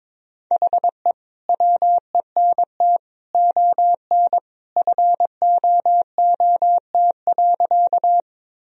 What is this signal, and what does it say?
Morse code, heard as a keyed tone, about 22 wpm, 700 Hz. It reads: HE WENT ON FOOT.
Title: Sentences from Top 1000 Words - Review Part 3 - 22wpm